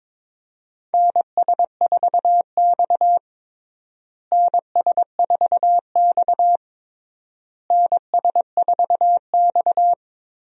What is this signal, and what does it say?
Morse code, heard as a keyed tone, about 22 wpm, 700 Hz. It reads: NS4X NS4X NS4X
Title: USA Call Signs - 4 Characters - Repeated 3x - 22wpm